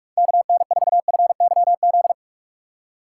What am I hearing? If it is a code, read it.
Morse code, heard as a keyed tone, about 47 wpm, 700 Hz. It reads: KN4FYZ
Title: USA Call Signs - 6 Characters - 6x Character Spacing - 45wpm